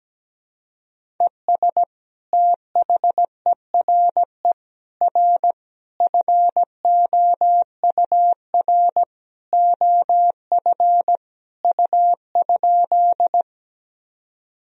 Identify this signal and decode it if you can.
Morse code, heard as a keyed tone, about 17 wpm, 700 Hz. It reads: ES THERE R FOUR OF U?
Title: Sentences from Top 300 Words and Abbreviations 17wpm